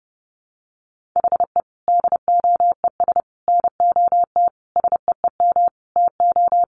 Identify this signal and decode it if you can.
Morse code, heard as a keyed tone, about 28 wpm, 700 Hz. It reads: HE DOES NOT SEEM TO
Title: Sentences from Top 1400 Words 30wpm